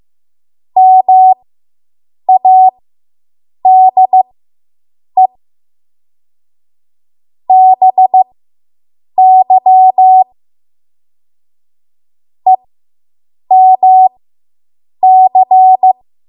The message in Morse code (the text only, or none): MADE BY EMC